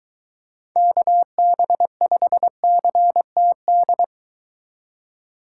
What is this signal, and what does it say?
Morse code, heard as a keyed tone, about 23 wpm, 700 Hz. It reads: KB5CTD